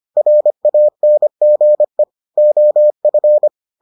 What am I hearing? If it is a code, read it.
Morse code, heard as a keyed tone, about 25 wpm, 600 Hz. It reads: RANGE OF